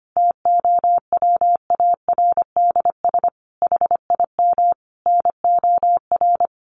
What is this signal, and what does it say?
Morse code, heard as a keyed tone, about 25 wpm, 700 Hz. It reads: TOWARDS HIM NOR
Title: Sentences from Top 1500 Words 25wpm